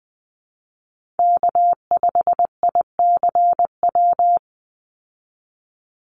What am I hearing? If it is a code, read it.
Morse code, heard as a keyed tone, about 20 wpm, 700 Hz. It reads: K5ICW
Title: USA Call Signs 20wpm